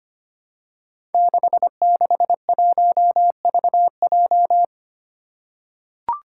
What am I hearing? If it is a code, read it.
Morse code, heard as a keyed tone, about 25 wpm, 700 Hz. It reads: 661VJ
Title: Five Challenging Letter-Number Combinations 25wpm